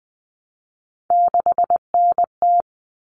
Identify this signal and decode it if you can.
Morse code, heard as a keyed tone, about 20 wpm, 700 Hz. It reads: 6NT
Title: All Cut Numbers - 3-Digit 20wpm